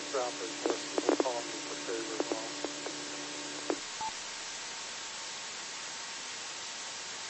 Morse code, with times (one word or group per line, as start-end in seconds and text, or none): none